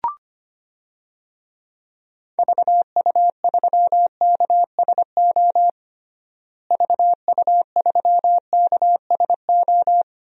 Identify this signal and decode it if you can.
Morse code, heard as a keyed tone, about 25 wpm, 700 Hz. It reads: VU3KSO VU3KSO
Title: DXCC-100 Call Signs - 6 Characters - Repeated 2x - 25wpm